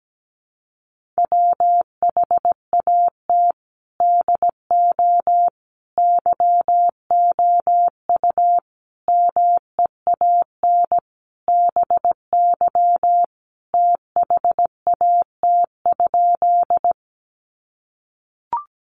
Morse code, heard as a keyed tone, about 17 wpm, 700 Hz. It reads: WHAT DO YOU MEAN BY THAT?
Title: Sentences from Top 200 Words 17wpm